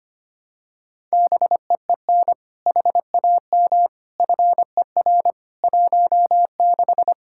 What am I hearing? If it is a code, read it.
Morse code, heard as a keyed tone, about 25 wpm, 700 Hz. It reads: BEEN HAM FER 16
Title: Practice QSO - Exchange 3 - More Info - 25wpm